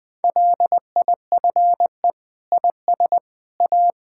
Morse code, heard as a keyed tone, about 20 wpm, 700 Hz. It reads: LIFE IS A
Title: Sentences from Top 1000 Words - Review Part 6 - 20wpm